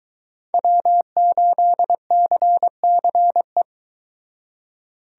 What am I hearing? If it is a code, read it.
Morse code, heard as a keyed tone, about 24 wpm, 700 Hz. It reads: W8CCE